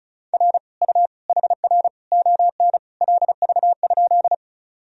Morse code, heard as a keyed tone, about 36 wpm, 700 Hz. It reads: R U HR ON LV?
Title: Sentences from Top 400 Words and Abbreviations 35wpm